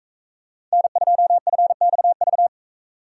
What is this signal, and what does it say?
Morse code, heard as a keyed tone, about 42 wpm, 700 Hz. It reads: N2FXV